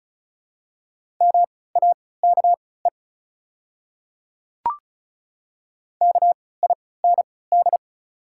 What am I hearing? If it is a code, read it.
Morse code, heard as a keyed tone, about 34 wpm, spaced as for 20 wpm, 700 Hz. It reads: MAKE KIND
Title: 3x Character Spacing - Top 300 Words - 35wpm